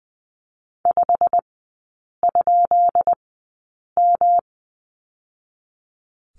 Morse code, heard as a keyed tone, about 19 wpm, spaced as for 8 wpm, 700 Hz. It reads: 5?M
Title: 133 Lesson 22e - Question Mark - Three Characters 20wpm